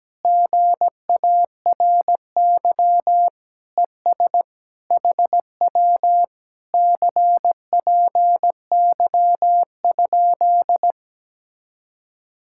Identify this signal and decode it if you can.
Morse code, heard as a keyed tone, about 17 wpm, 700 Hz. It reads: GARY ES HW CPY?
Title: Speed-Racing Common QSO Phrases - 17wpm